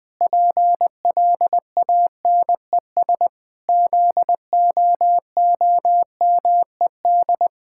Morse code, heard as a keyed tone, about 20 wpm, 700 Hz. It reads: PLANES ZOOMED
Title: Pangram Challenge v2 - 20wpm